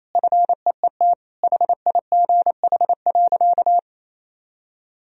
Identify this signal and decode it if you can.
Morse code, heard as a keyed tone, about 28 wpm, 700 Hz. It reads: FEET HIGH.
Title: Sentences from Top 1000 Words - Review Part 1 - 28wpm